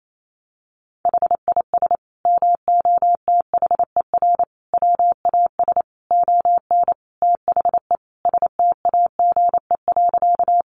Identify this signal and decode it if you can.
Morse code, heard as a keyed tone, about 28 wpm, 700 Hz. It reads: HIS MOTHER WAS ON THE STAGE.
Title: Sentences from Top 1200 Words 28wpm